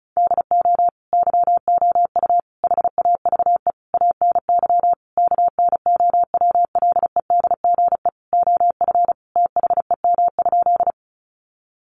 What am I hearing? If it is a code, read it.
Morse code, heard as a keyed tone, about 34 wpm, 700 Hz. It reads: DO YOU HAVE ANY KNOWLEDGE OF THEM?
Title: Sentences from Top 600 Words 35wpm